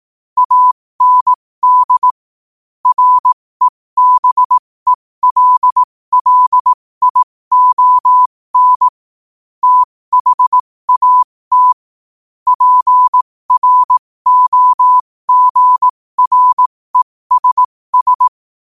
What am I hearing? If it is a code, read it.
Morse code, heard as a keyed tone, about 18 wpm, spaced as for 15 wpm, 1000 Hz. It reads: AND REBELLION THAT PROGRESS
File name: qotdfiles / 15wpm